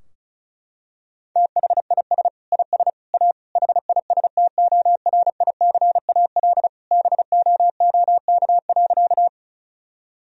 Morse code, heard as a keyed tone, about 36 wpm, 700 Hz. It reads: THIS IS A HISTORICAL BOOK.
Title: Sentences from Top 1000 Words - Review Part 4 - 35wpm